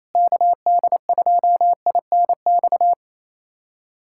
Morse code, heard as a keyed tone, about 28 wpm, 700 Hz. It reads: KD2INX
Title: USA Callsign DE Callsign 28wpm